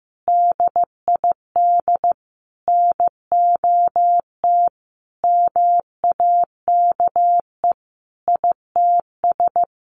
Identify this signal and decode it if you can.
Morse code, heard as a keyed tone, about 15 wpm, 700 Hz. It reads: DID NOT MAKE ITS
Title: Sentences from Top 1100 Words 15wpm